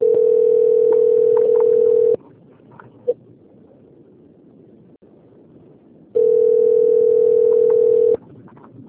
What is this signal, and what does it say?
Ringback tone (North America), 440 + 480 Hz.
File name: out- - - - - .72646